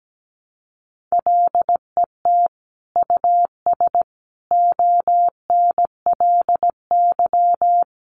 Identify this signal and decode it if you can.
Morse code, heard as a keyed tone, about 17 wpm, 700 Hz. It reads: LET US ONLY
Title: Sentences from Top 800 Words 17wpm